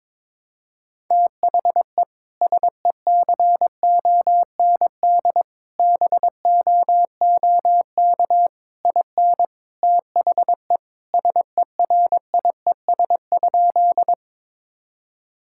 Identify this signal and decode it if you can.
Morse code, heard as a keyed tone, about 22 wpm, 700 Hz. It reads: THE SECOND BOOK IN THE SERIES?
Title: Sentences from Top 1000 Words - Review Part 5 - 22wpm